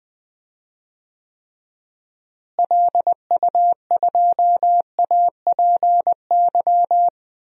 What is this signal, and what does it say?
Morse code, heard as a keyed tone, about 20 wpm, 700 Hz. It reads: LU2APY
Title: DXCC-100 Call Signs - 6 Characters - Repeated 2x - 20wpm